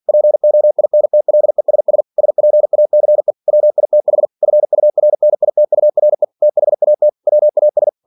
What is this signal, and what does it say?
Morse code, heard as a keyed tone, about 47 wpm, 600 Hz. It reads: POINTLESS SPACE WITH FURNITURE THAT WAS
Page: Never 47wpm